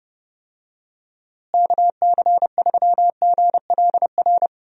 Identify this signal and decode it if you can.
Morse code, heard as a keyed tone, about 30 wpm, 700 Hz. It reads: KC3GLR
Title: USA Call Signs - 6 Characters - Repeated 3x - 30wpm